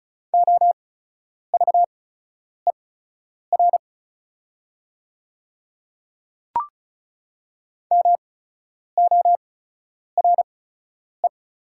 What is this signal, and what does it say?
Morse code, heard as a keyed tone, about 36 wpm, spaced as for 10 wpm, 700 Hz. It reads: OVER MORE
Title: 8x Character Spacing - Top 100 Words - 35wpm